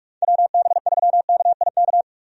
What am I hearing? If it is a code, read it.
Morse code, heard as a keyed tone, about 46 wpm, 700 Hz. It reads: WB3KIK
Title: USA Call Signs - 6 Characters - 45wpm